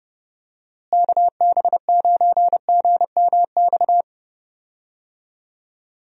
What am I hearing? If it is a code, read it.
Morse code, heard as a keyed tone, about 31 wpm, 700 Hz. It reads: KB9GMX
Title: USA Call Signs - 6 Characters - Repeated 3x - 30wpm